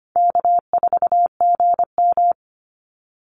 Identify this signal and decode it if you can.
Morse code, heard as a keyed tone, about 25 wpm, 700 Hz. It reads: K4GM
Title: Call Signs - 25wpm